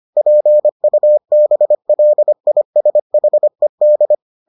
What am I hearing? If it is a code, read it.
Morse code, heard as a keyed tone, about 25 wpm, 600 Hz. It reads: PUBLISHED